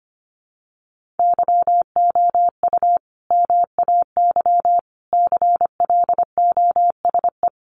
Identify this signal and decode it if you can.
Morse code, heard as a keyed tone, about 25 wpm, 700 Hz. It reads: YOU MAY CLOSE
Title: Sentences from Top 1500 Words 25wpm